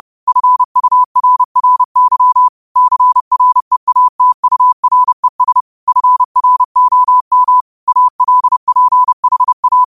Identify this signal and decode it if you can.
Morse code, heard as a keyed tone, about 30 wpm, 1000 Hz. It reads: FURRY CREATURES FROM ALPHA
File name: qotdfiles / 30wpm